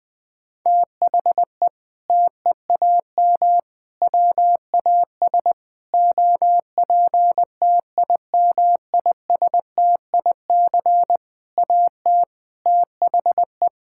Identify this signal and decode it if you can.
Morse code, heard as a keyed tone, about 20 wpm, 700 Hz. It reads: THE TEAM WAS OPTIMISTIC AT THE
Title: Sentences with 6-Word Phrases - 20wpm